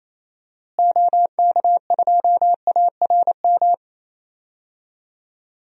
Morse code, heard as a keyed tone, about 28 wpm, 700 Hz. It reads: OK2ARM
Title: DXCC-100 Call Signs - 6 Characters - Repeated 3x - 28wpm